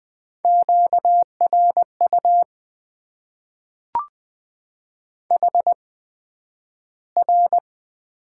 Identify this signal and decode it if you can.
Morse code, heard as a keyed tone, about 20 wpm, spaced as for 11 wpm, 700 Hz. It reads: QRU HR
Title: Top 90 QSO Elements - 8x Character Spacing - 20wpm